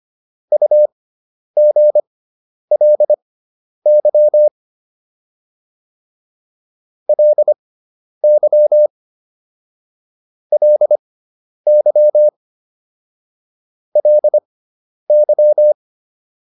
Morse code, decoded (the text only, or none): UGLY LY LY LY